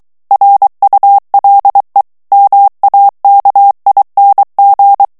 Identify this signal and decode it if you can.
Morse code, heard as a keyed tone, about 23 wpm, 800 Hz. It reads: RULE MAKING
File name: ANGLAIS texte clair 3 vitesse 22 mots minute